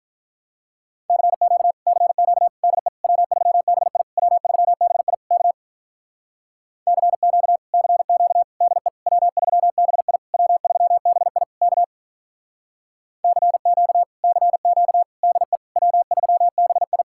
Morse code, heard as a keyed tone, about 48 wpm, 700 Hz. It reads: CQ CQ DE W3BI W3BI K CQ CQ DE W3BI W3BI K CQ CQ DE W3BI